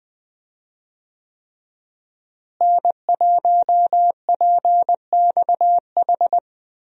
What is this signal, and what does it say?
Morse code, heard as a keyed tone, about 20 wpm, 700 Hz. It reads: N1PXH
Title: USA Call Signs - 5 Characters - Repeated 3x - 20wpm